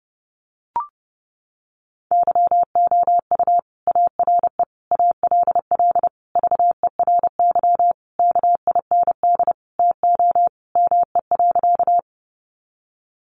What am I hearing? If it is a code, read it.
Morse code, heard as a keyed tone, about 28 wpm, 700 Hz. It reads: YOU ARE ALL VERY KIND TO ME.
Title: Sentences from Top 200 Words 30wpm